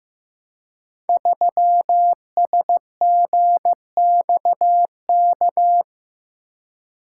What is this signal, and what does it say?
Morse code, heard as a keyed tone, about 15 wpm, 700 Hz. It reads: 3SGXK
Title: Code Groups - 5 Letter:Numbers - 15wpm